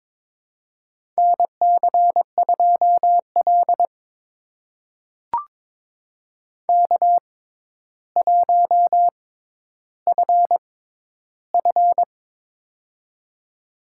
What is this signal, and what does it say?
Morse code, heard as a keyed tone, about 22 wpm, spaced as for 11 wpm, 700 Hz. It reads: NC2L K1FF
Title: USA Call Signs - 4 Characters - 6x Character Spacing - 22wpm